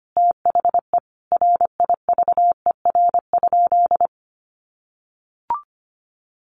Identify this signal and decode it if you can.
Morse code, heard as a keyed tone, about 25 wpm, 700 Hz. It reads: THE RIVER?
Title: Sentences from Top 1400 Words 25wpm